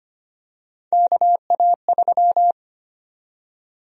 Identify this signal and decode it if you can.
Morse code, heard as a keyed tone, about 25 wpm, 700 Hz. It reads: KA3
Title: USA Call Sign Prefixes - 25wpm